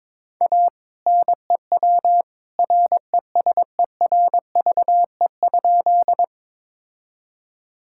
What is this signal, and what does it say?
Morse code, heard as a keyed tone, about 22 wpm, 700 Hz. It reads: A NEW RESERVE?
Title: Sentences from Top 1000 Words - Review Part 6 - 22wpm